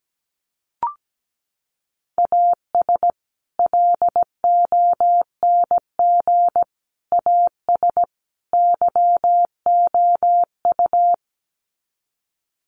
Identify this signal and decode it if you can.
Morse code, heard as a keyed tone, about 17 wpm, 700 Hz. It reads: AS LONG AS YOU
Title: Sets of 4 Words 17wpm